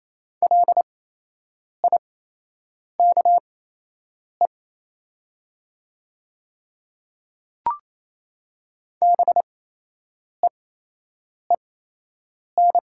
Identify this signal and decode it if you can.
Morse code, heard as a keyed tone, about 28 wpm, spaced as for 8 wpm, 700 Hz. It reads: LIKE BEEN
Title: 8x Character Spacing - Top 100 Words - 28wpm